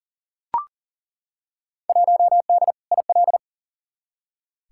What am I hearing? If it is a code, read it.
Morse code, heard as a keyed tone, about 40 wpm, 700 Hz. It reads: 1D IL
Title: Field Day Exchanges - 40wpm